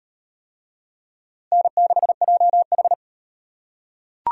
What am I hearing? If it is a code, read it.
Morse code, heard as a keyed tone, about 34 wpm, 700 Hz. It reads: N6JH